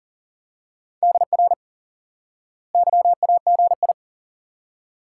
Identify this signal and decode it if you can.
Morse code, heard as a keyed tone, about 37 wpm, 700 Hz. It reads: DR YAGI
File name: IQE Course 3 - Lesson 89 - YAGI - Two Elements - 40wpm